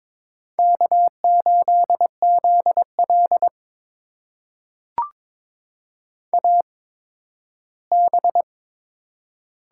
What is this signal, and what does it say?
Morse code, heard as a keyed tone, about 22 wpm, spaced as for 13 wpm, 700 Hz. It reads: K8ZL AB
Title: USA Call Signs - 4 Characters - 8x Character Spacing - 22wpm